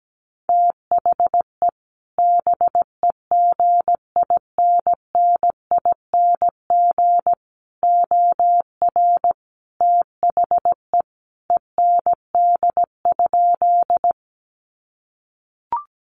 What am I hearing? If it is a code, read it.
Morse code, heard as a keyed tone, about 17 wpm, 700 Hz. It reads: THE BEGINNING OR THE END?
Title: Sentences from Top 800 Words 17wpm